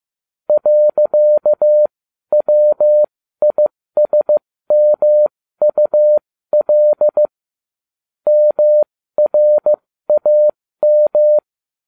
Morse code, heard as a keyed tone, about 15 wpm, spaced as for 13 wpm, 600 Hz. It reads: .WISMUL MRAM